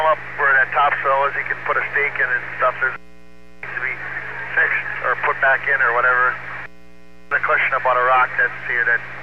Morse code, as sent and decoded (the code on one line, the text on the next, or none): none